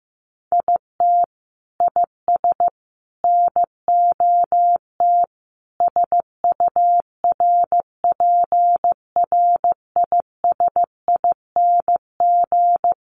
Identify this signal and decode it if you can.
Morse code, heard as a keyed tone, about 15 wpm, 700 Hz. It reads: IT IS NOT SURPRISING
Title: Sets of 5 Words 15wpm